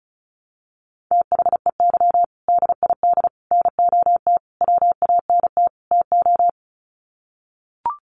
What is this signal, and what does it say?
Morse code, heard as a keyed tone, about 34 wpm, 700 Hz. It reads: THEY DID NOT WANT TO
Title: Sets of 5 Words 35wpm